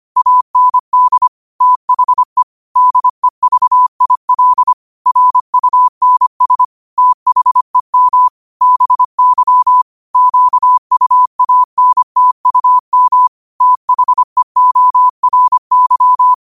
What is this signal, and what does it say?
Morse code, heard as a keyed tone, about 25 wpm, 1000 Hz. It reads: AND THE DEVIL RUNS THEM BY QUANTUM THEORY
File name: qotdfiles / 25wpm